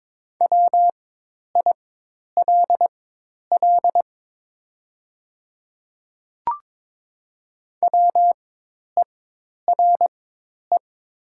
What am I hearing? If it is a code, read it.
Morse code, heard as a keyed tone, about 22 wpm, spaced as for 10 wpm, 690 Hz. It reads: WILL WERE
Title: 4x Character Spacing - Top 100 Words - 22wpm